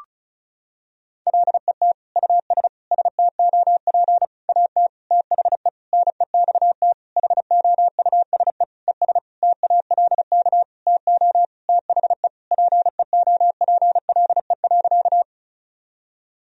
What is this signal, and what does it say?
Morse code, heard as a keyed tone, about 34 wpm, 700 Hz. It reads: LET US STOP AT THE NEXT HOUSE ES TALK TO THE PEOPLE.
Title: Sentences from Top 400 Words and Abbreviations 35wpm